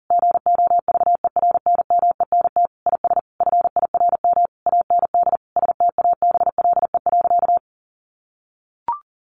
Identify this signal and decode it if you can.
Morse code, heard as a keyed tone, about 40 wpm, 700 Hz. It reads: GOVERNMENT IS FIRM AND STABLE.
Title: Sentences from Top 1500 Words 40wpm